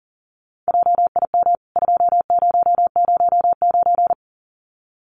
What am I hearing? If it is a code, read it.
Morse code, heard as a keyed tone, about 40 wpm, 700 Hz. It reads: JIM 2009
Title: CWT Exchange 40wpm